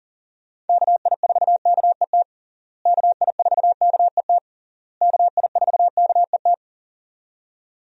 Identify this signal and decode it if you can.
Morse code, heard as a keyed tone, about 37 wpm, 700 Hz. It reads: KI4KET KI4KET KI4KET
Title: USA Call Signs - 6 Characters - Repeated 3x - 40wpm